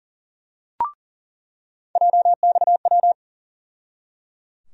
Morse code, heard as a keyed tone, about 38 wpm, 700 Hz. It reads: JXW
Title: USA Call Sign Postfixes - 3 Letters 40wpm